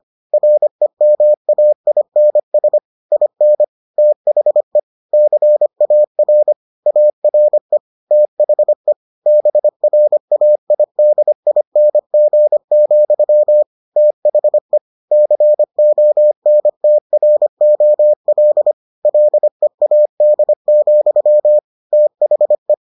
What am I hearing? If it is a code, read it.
Morse code, heard as a keyed tone, about 25 wpm, 600 Hz. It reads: REMAINS IN THE CAR ARE THE BRAIDING, THE CONTROL LEAD, THE